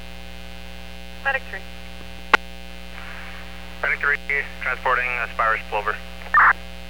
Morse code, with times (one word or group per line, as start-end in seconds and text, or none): none